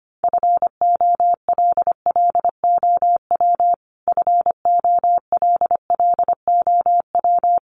Morse code, heard as a keyed tone, about 25 wpm, 700 Hz. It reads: FOLLOW FOLLOW
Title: Top 500 Words Repeated 25wpm